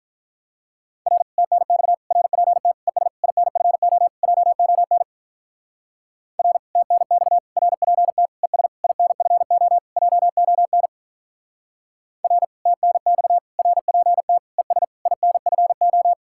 Morse code, heard as a keyed tone, about 43 wpm, 700 Hz. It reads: R TNX RPT ES INFO JON R TNX RPT ES INFO JON R TNX RPT ES INFO
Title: Speed-Racing Common QSO Phrases - 35wpm